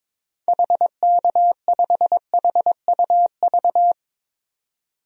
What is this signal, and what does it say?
Morse code, heard as a keyed tone, about 22 wpm, 700 Hz. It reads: HK5HUV
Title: DXCC-100 Call Signs - 6 Characters - 22wpm